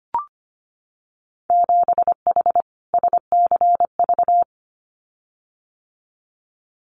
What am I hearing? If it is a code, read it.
Morse code, heard as a keyed tone, about 25 wpm, 700 Hz. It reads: 7H SCV
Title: Winter Field Day Exchanges Encore 25wpm